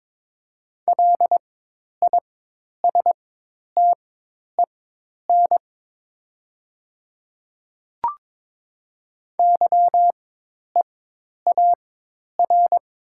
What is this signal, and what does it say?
Morse code, heard as a keyed tone, about 22 wpm, spaced as for 10 wpm, 700 Hz. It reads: LISTEN YEAR
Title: 4x Character Spacing - Top 500 Words - 22wpm